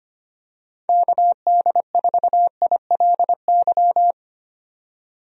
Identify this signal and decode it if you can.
Morse code, heard as a keyed tone, about 25 wpm, 700 Hz. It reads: KD4ILY
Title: USA Call Signs - 6 Characters - 25wpm